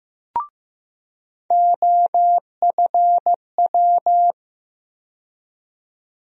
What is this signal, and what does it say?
Morse code, heard as a keyed tone, about 15 wpm, 700 Hz. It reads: OFW
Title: USA Call Sign Postfixes - 3 Letters 15wpm